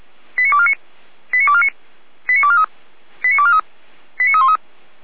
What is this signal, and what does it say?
Five-tone selective call (ZVEI), heard as quick runs of five tones, tones of about 70 ms. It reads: 89249 89249 89243 89243 89313